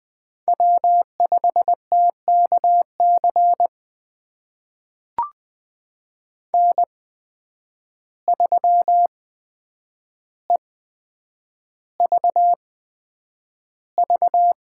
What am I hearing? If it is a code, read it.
Morse code, heard as a keyed tone, about 20 wpm, spaced as for 9 wpm, 700 Hz. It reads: W5TKC N3EVV
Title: USA Call Signs - 5 Characters - 8x Character Spacing - 20wpm